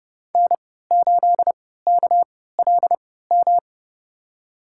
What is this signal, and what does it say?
Morse code, heard as a keyed tone, about 30 wpm, spaced as for 17 wpm, 700 Hz. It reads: N8KLM
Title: USA Call Signs - 5 Characters - 3x Character Spacing - 30wpm